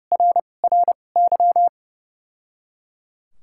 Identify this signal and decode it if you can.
Morse code, heard as a keyed tone, about 32 wpm, spaced as for 22 wpm, 700 Hz. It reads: RRY
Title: 2x Character Spacing - Top 300 Words - 30wpm